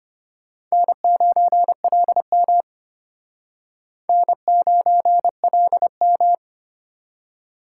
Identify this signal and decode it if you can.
Morse code, heard as a keyed tone, about 31 wpm, 700 Hz. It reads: N9LM N9LM